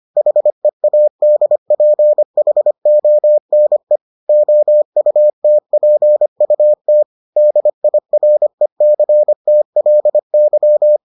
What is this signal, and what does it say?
Morse code, heard as a keyed tone, about 25 wpm, 600 Hz. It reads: HEADPHONE OUTPUT DIRECTLY